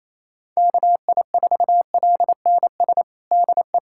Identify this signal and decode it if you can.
Morse code, heard as a keyed tone, about 28 wpm, 700 Hz. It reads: KI4LNS DE
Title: USA Callsign DE Callsign 28wpm